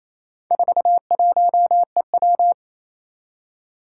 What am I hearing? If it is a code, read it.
Morse code, heard as a keyed tone, about 28 wpm, 700 Hz. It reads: 41EW
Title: Four Challenging Letter-Number Combinations 28wpm